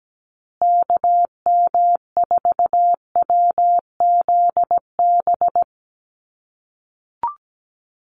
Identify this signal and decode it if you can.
Morse code, heard as a keyed tone, about 17 wpm, 700 Hz. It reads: KM4WZB
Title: USA Call Signs Repeated 3x 17wpm